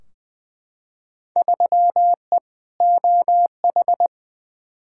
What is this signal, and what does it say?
Morse code, heard as a keyed tone, about 20 wpm, 700 Hz. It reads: 3E OH
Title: Field Day Exchanges - 20wpm